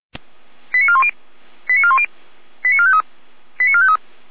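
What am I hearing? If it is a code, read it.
Five-tone selective call (ZVEI), heard as quick runs of five tones, tones of about 70 ms. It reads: 89410 89410 89453 89453
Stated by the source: ZVEI five-tone selective call received on SDR.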